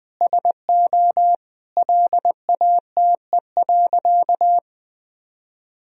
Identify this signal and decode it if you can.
Morse code, heard as a keyed tone, about 20 wpm, 700 Hz. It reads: SO LATE.
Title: Sentences from Top 1000 Words - Review Part 3 - 20wpm